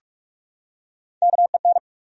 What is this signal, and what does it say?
Morse code, heard as a keyed tone, about 45 wpm, 700 Hz. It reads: KEN
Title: USA Common Names - Max 4 Letters - 45wpm